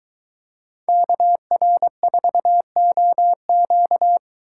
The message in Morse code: KR4OQ

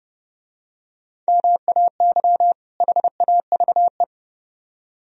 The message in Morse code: MAY HAVE

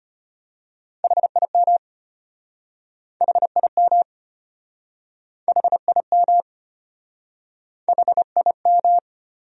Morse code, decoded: HIM HIM HIM HIM

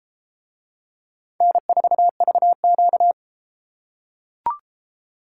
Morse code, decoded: N4VQ

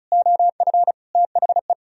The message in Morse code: OF THE

